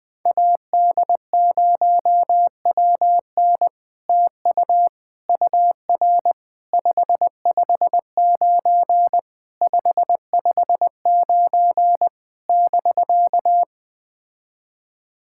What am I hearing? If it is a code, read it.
Morse code, heard as a keyed tone, about 20 wpm, 700 Hz. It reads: AD0WN TU UR 559 559 <BK>
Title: POTA QSO Practice - 20wpm